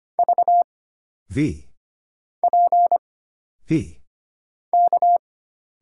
Morse code, heard as a keyed tone, about 25 wpm, 700 Hz. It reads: V P K
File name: Single Character (Letter:Number:Punctuation:Prosign) - Mind-Melt - 25wpm